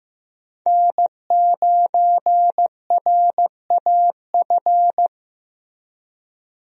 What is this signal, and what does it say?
Morse code, heard as a keyed tone, about 15 wpm, 700 Hz. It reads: N9RAF
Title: USA Call Signs - 5 Characters - Repeated 2x - 15wpm